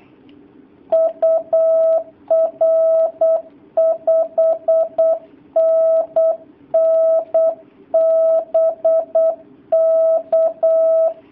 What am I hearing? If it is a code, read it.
Morse code, heard as a keyed tone, about 8 wpm, 650 Hz. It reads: UR 5NN BK
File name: jh8cbh / 2019cw / sp08- w